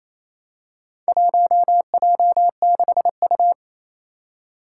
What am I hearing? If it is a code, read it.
Morse code, heard as a keyed tone, about 27 wpm, 700 Hz. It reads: 1J6U
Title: Four Challenging Letter-Number Combinations 28wpm